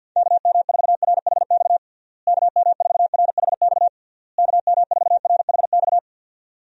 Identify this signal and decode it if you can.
Morse code, heard as a keyed tone, about 52 wpm, 700 Hz. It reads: KM4RHX KM4RHX KM4RHX
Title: USA Call Signs - 6 Characters - Repeated 3x - 50wpm